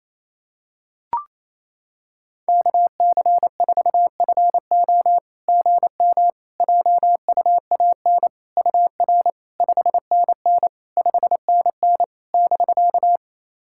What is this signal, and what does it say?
Morse code, heard as a keyed tone, about 28 wpm, 700 Hz. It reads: KC4FO GM JUAN UR 5NN 5NN <BK>
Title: SOTA QSO Practice 28wpm